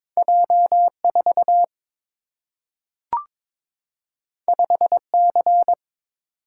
Morse code, heard as a keyed tone, about 22 wpm, 700 Hz. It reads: J4 5C
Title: Single-Letter Single-Number 22wpm